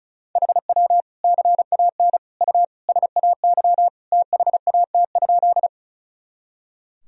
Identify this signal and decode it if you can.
Morse code, heard as a keyed tone, about 36 wpm, 700 Hz. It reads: HW CAN U SAY THAT?
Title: Sentences from Top 200 Words and Abbreviations 35wpm